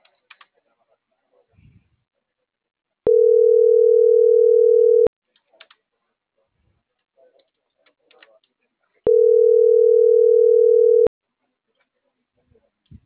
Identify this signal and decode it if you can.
Ringback tone (North America), 440 + 480 Hz, with speech.
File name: CENTRAL MABE